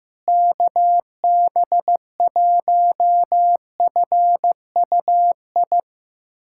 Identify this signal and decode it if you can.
Morse code, heard as a keyed tone, about 15 wpm, 700 Hz. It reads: KB1FUI
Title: USA Callsign DE Callsign - 6 Characters - 15wpm